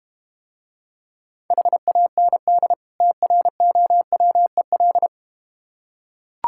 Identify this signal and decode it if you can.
Morse code, heard as a keyed tone, about 32 wpm, 700 Hz. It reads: HAND TROWEL